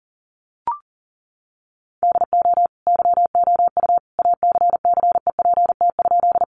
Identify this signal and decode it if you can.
Morse code, heard as a keyed tone, about 40 wpm, 700 Hz. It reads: DO YOU ACCEPT?
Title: Sentences from Top 1300 Words 40wpm